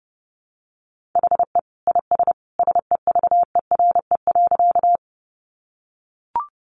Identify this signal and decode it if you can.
Morse code, heard as a keyed tone, about 30 wpm, 700 Hz. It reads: HE IS SEVERE.
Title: Sentences from Top 1400 Words 30wpm